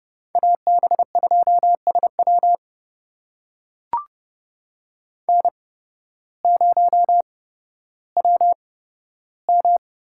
Morse code, heard as a keyed tone, about 30 wpm, spaced as for 14 wpm, 700 Hz. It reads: AB2SW N0WM
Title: USA Call Signs - 5 Characters - 8x Character Spacing - 30wpm